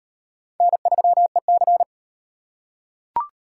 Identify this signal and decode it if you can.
Morse code, heard as a keyed tone, about 38 wpm, 700 Hz. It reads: N3EC